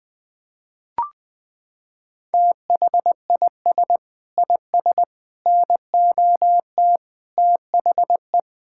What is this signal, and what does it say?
Morse code, heard as a keyed tone, about 20 wpm, 700 Hz. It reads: THIS IS NOT THE
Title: Sentences with 8-Word Phrases - 20wpm